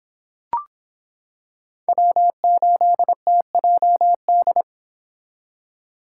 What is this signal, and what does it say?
Morse code, heard as a keyed tone, about 26 wpm, 700 Hz. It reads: W8TJD